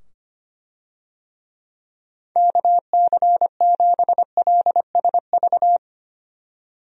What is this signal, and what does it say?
Morse code, heard as a keyed tone, about 25 wpm, 700 Hz. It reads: KC7LSV
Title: USA Call Signs - 6 Characters - 25wpm